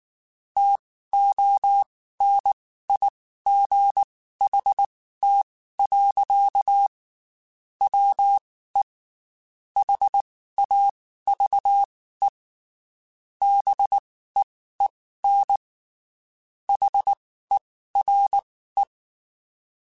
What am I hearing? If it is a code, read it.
Morse code, heard as a keyed tone, about 19 wpm, spaced as for 14 wpm, 800 Hz. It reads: TONIGHT. WE HAVE BEEN HERE
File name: QSOs / QSOs-mp3s / WD0BGP 1@18